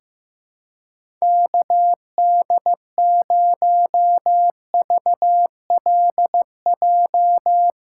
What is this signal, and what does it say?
Morse code, heard as a keyed tone, about 16 wpm, 700 Hz. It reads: KD0VLJ
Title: USA Callsign DE Callsign - 6 Characters - 15wpm